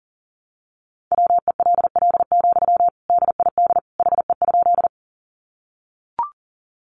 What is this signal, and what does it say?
Morse code, heard as a keyed tone, about 40 wpm, 700 Hz. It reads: WELL, DID HE?
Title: Sentences from Top 200 Words 40wpm